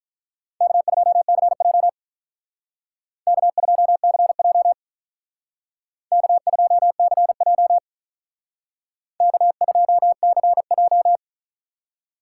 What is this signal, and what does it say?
Morse code, heard as a keyed tone, about 43 wpm, 700 Hz. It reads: K2CJ K2CJ K2CJ K2CJ
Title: Speed Racing - USA 1x2 Call Signs 35wpm